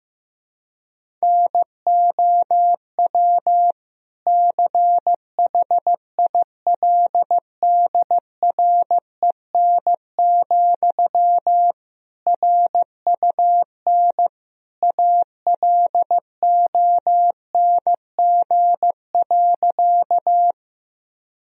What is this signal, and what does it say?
Morse code, heard as a keyed tone, about 15 wpm, 700 Hz. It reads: NOW CHILDREN, RUN ALONG.
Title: Sentences from Top 1000 Words - Review Part 1 - 15wpm